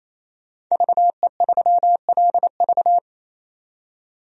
Morse code, heard as a keyed tone, about 28 wpm, 700 Hz. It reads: VE3LV
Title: DXCC-100 Call Signs - 5 Characters - 28wpm